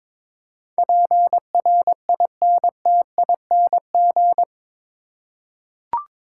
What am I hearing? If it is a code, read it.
Morse code, heard as a keyed tone, about 22 wpm, 700 Hz. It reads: PRINTING
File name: practice / 22wpm / Eight Letter Words - 22wpm